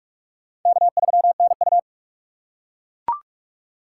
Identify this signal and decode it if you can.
Morse code, heard as a keyed tone, about 43 wpm, 700 Hz. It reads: K3NU
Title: USA Call Signs - 4 Characters - 4x Character Spacing - 45wpm